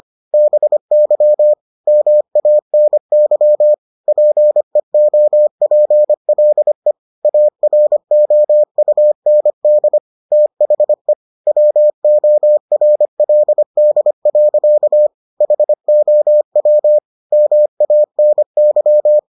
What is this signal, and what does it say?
Morse code, heard as a keyed tone, about 25 wpm, 600 Hz. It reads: BY MANY PEOPLE AROUND THE WORLD. HOW MANY